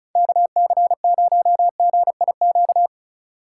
Morse code, heard as a keyed tone, about 36 wpm, 700 Hz. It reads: KC0GIQ
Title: USA Call Signs - 6 Characters - 2x Character Spacing - 35wpm